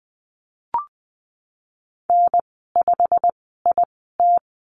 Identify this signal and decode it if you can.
Morse code, heard as a keyed tone, about 20 wpm, spaced as for 15 wpm, 700 Hz. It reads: N5IT
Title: 088 Lesson 15g - Number 5 - Call Signs 20wpm